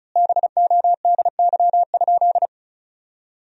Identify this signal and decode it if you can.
Morse code, heard as a keyed tone, about 36 wpm, 700 Hz. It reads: BODY?
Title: 3x Word-Spacing - Sentences from Top 400 Words - 35wpm